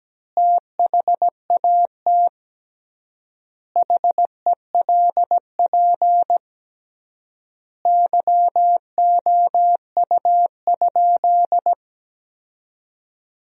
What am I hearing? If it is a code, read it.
Morse code, heard as a keyed tone, about 17 wpm, 700 Hz. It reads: THAT HELP YOU?
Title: 3x Word-Spacing - Sentences from Top 200 Words - 17wpm